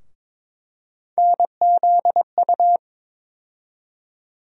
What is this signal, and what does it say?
Morse code, heard as a keyed tone, about 22 wpm, 700 Hz. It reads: NZU
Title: USA Call Sign Postfixes - 3 Letters 22wpm